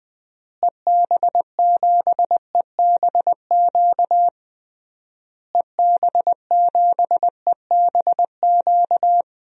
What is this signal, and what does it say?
Morse code, heard as a keyed tone, about 20 wpm, 700 Hz. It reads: EB7EBQ EB7EBQ
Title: DXCC-100 Call Signs - 6 Characters - Repeated 3x - 20wpm